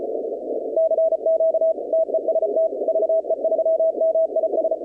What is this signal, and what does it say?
Morse code, heard as a keyed tone, about 35 wpm, 660 Hz, with noise. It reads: CQ TEST VE3MIS
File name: contests / 2019-cw-wpx-kt5j-k5pi- -2330 0-21